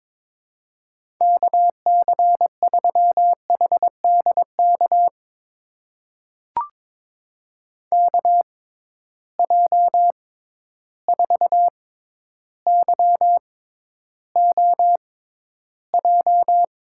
Morse code, heard as a keyed tone, about 22 wpm, spaced as for 11 wpm, 700 Hz. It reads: KC3HDK KJ4YOJ
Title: USA Call Signs - 6 Characters - 6x Character Spacing - 22wpm